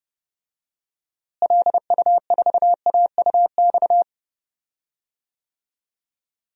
Morse code, heard as a keyed tone, about 32 wpm, 700 Hz. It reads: LU4AUX
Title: DXCC-100 Call Signs - 6 Characters - Repeated 2x - 30wpm